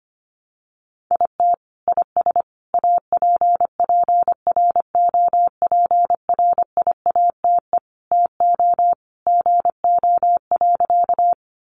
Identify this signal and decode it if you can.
Morse code, heard as a keyed tone, about 24 wpm, 700 Hz. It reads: IT IS APPROPRIATE TO GO.
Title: Sentences from Top 900 Words 25wpm